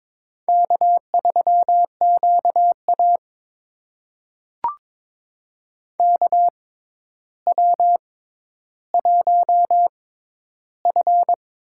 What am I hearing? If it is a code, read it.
Morse code, heard as a keyed tone, about 22 wpm, spaced as for 11 wpm, 700 Hz. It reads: K3QA KW1F
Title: USA Call Signs - 4 Characters - 6x Character Spacing - 22wpm